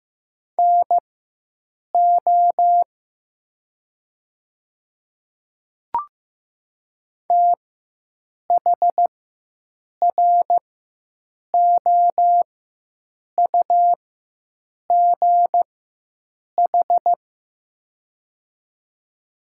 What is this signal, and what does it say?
Morse code, heard as a keyed tone, about 15 wpm, spaced as for 7 wpm, 700 Hz. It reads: NO THROUGH
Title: 4x Character Spacing - Top 300 Words - 15wpm